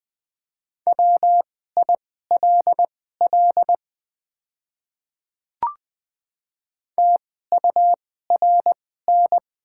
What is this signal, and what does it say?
Morse code, heard as a keyed tone, about 20 wpm, spaced as for 14 wpm, 700 Hz. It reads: WILL TURN
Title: 2x Character Spacing - Top 200 Words - 20wpm